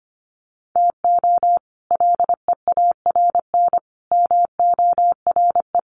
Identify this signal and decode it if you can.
Morse code, heard as a keyed tone, about 25 wpm, 700 Hz. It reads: TO LEARN MORE
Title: Sets of 5 Words Encore2 25wpm